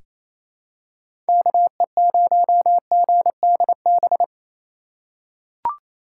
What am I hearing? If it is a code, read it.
Morse code, heard as a keyed tone, about 28 wpm, 700 Hz. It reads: KE0GDB
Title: USA Call Signs - 6 Characters - 2x Character Spacing - 28wpm